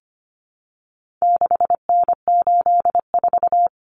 Morse code, heard as a keyed tone, about 25 wpm, 700 Hz. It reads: 6N84
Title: Common Cut Numbers - 4-Digit 25wpm